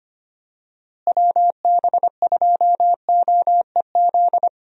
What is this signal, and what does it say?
Morse code, heard as a keyed tone, about 25 wpm, 700 Hz. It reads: WB2OEZ
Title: USA Call Signs - 6 Characters - Repeated 2x - 25wpm